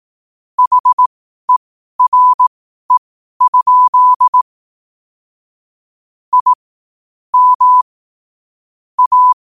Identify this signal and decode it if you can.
Morse code, heard as a keyed tone, about 18 wpm, spaced as for 12 wpm, 1000 Hz. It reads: HERE? I M A
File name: qotdfiles / 12wpm